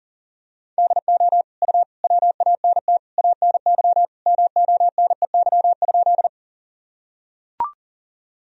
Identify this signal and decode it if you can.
Morse code, heard as a keyed tone, about 37 wpm, 700 Hz. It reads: DO U WANT ANY MONEY?
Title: Sentences from Top 400 Words and Abbreviations 40wpm